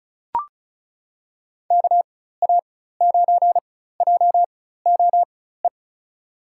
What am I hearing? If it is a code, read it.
Morse code, heard as a keyed tone, about 34 wpm, spaced as for 16 wpm, 700 Hz. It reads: KA9JOE
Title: USA Call Signs - 4x Character Spacing - 35wpm